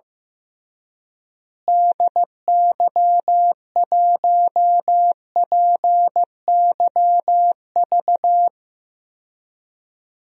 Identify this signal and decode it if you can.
Morse code, heard as a keyed tone, about 15 wpm, 700 Hz. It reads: DY1PYV
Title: DXCC-100 Call Signs - 6 Characters - Repeated 3x - 15wpm